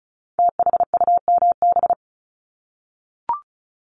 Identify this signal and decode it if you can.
Morse code, heard as a keyed tone, about 36 wpm, 700 Hz. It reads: THUMB
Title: Multi-letter Phonemes 35wpm